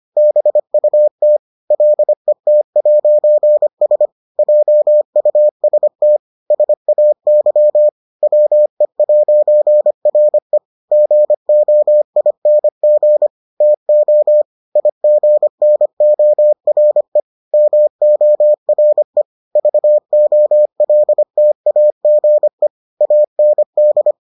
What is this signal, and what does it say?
Morse code, heard as a keyed tone, about 25 wpm, 600 Hz. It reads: BUT LET'S JUST SAY WE'RE GOING TO IGNORE MORE VOLTAGE AND